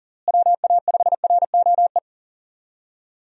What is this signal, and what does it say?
Morse code, heard as a keyed tone, about 39 wpm, 700 Hz. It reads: WA5ROE
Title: USA Call Signs - 6 Characters - Repeated 3x - 40wpm